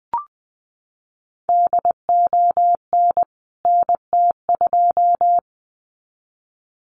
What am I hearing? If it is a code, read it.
Morse code, heard as a keyed tone, about 20 wpm, 700 Hz. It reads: DON NT2
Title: CWT Exchange 20wpm